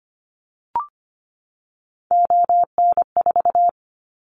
Morse code, heard as a keyed tone, about 25 wpm, 700 Hz. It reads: ON4
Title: Callsign Prefixes for Top 50 DXCC - 25wpm